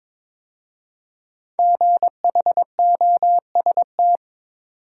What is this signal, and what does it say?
Morse code, heard as a keyed tone, about 23 wpm, 700 Hz. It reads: GHOST
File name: 5 Letter Words - First 3 Letters Given - 22wpm